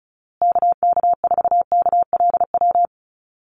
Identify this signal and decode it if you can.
Morse code, heard as a keyed tone, about 34 wpm, 700 Hz. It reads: KK4KLW
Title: USA Call Signs Repeated 3x 35wpm